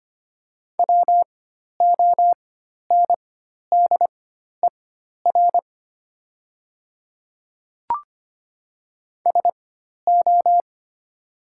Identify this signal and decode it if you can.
Morse code, heard as a keyed tone, about 25 wpm, spaced as for 12 wpm, 690 Hz. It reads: WONDER SO
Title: 4x Character Spacing - Top 500 Words - 25wpm